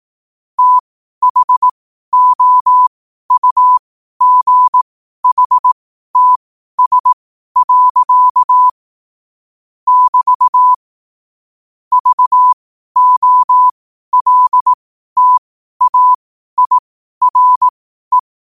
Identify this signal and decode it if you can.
Morse code, heard as a keyed tone, about 18 wpm, spaced as for 12 wpm, 1000 Hz. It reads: THOUGHTS. <BT> VOLTAIRE
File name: qotdfiles / 12wpm